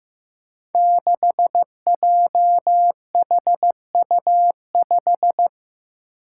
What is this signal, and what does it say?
Morse code, heard as a keyed tone, about 15 wpm, 700 Hz. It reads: 6JHU5
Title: Five Challenging Letter-Number Combinations 15wpm